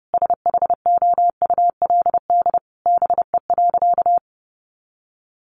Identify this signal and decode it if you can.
Morse code, heard as a keyed tone, about 30 wpm, 700 Hz. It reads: SHOULD BE.
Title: Sentences from Top 700 Words 30wpm